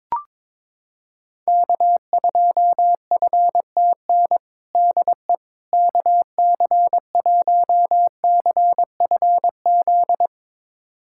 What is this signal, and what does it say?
Morse code, heard as a keyed tone, about 22 wpm, 700 Hz. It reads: K2FTN DE KC1CFZ
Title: USA Callsign DE Callsign 22wpm